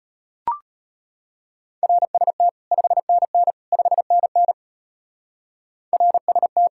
Morse code, heard as a keyed tone, about 33 wpm, 700 Hz. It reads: RST 5NN 5NN RST